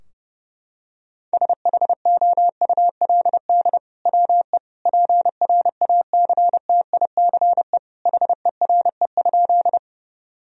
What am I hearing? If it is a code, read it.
Morse code, heard as a keyed tone, about 30 wpm, 700 Hz. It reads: SHOULD WE PRACTICE HERE?
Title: Sentences from Top 1000 Words - Review Part 4 - 30wpm